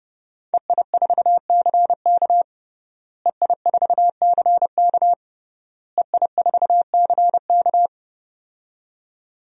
Morse code, heard as a keyed tone, about 30 wpm, 700 Hz. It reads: EI4CK EI4CK EI4CK
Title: DXCC-100 Call Signs - 5 Characters - Repeated 3x - 30wpm